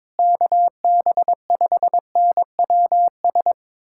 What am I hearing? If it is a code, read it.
Morse code, heard as a keyed tone, about 22 wpm, 700 Hz. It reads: KB5NWS
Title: USA Callsign DE Callsign 22wpm